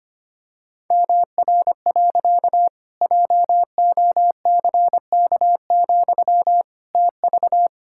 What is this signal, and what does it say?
Morse code, heard as a keyed tone, about 25 wpm, 700 Hz. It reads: MR. JOCK, TV
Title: Pangram Challenge v2 - 25wpm